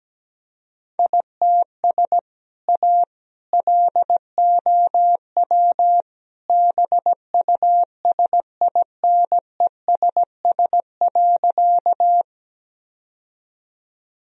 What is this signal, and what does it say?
Morse code, heard as a keyed tone, about 17 wpm, 700 Hz. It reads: ITS A LOW BUSINESS.
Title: Sentences from Top 1000 Words - Review Part 1 - 17wpm